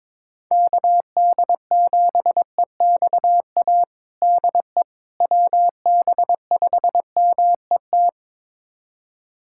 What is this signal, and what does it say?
Morse code, heard as a keyed tone, about 22 wpm, 700 Hz. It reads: KD7EXA DE WB5MET
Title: USA Callsign DE Callsign - 6 Characters - 22wpm